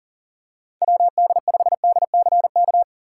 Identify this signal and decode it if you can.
Morse code, heard as a keyed tone, about 40 wpm, 700 Hz. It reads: WD5DCK
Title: USA Call Signs - 6 Characters - 6x Character Spacing - 40wpm